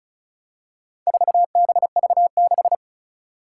Morse code, heard as a keyed tone, about 36 wpm, 700 Hz. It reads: 4BV6
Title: Four Challenging Letter-Number Combinations 35wpm